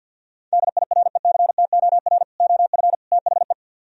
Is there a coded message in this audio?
Yes: Morse code from 0.5 to 3.5 s, heard as a keyed tone, about 64 wpm, 700 Hz. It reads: DIRECTOR OF THE